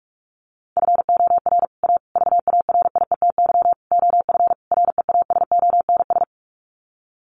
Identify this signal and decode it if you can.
Morse code, heard as a keyed tone, about 44 wpm, 700 Hz. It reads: FOR A VARIETY OF REASONS
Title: Sets of 5 Words Encore 45wpm